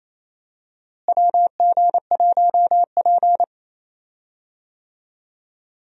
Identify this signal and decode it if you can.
Morse code, heard as a keyed tone, about 28 wpm, 700 Hz. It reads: WG1P
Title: USA Call Signs - 4 Characters - Repeated 2x - 28wpm